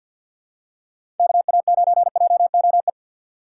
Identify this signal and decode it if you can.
Morse code, heard as a keyed tone, about 52 wpm, 700 Hz. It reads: KA9JOE
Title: USA Call Signs - 3x Character Spacing - 50wpm